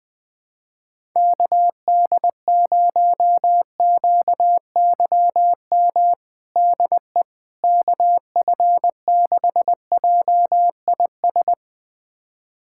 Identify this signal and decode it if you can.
Morse code, heard as a keyed tone, about 21 wpm, 700 Hz. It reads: KD0QYM DE KF6JIS
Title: USA Callsign DE Callsign - 6 Characters - 20wpm